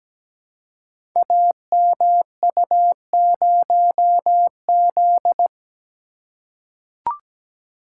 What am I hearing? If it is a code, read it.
Morse code, heard as a keyed tone, about 17 wpm, 700 Hz. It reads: AMU0Z
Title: Code Groups - 5 Letter:Numbers - 17wpm